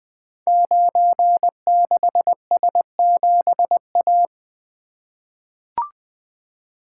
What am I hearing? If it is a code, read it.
Morse code, heard as a keyed tone, about 20 wpm, 700 Hz. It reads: 96S7A
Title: Code Groups - 5 Letter:Numbers - 20wpm